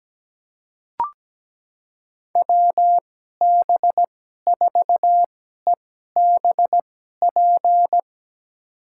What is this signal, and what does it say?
Morse code, heard as a keyed tone, about 17 wpm, spaced as for 12 wpm, 700 Hz. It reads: WB4EBP
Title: USA Call Signs - 2x Character Spacing - 17wpm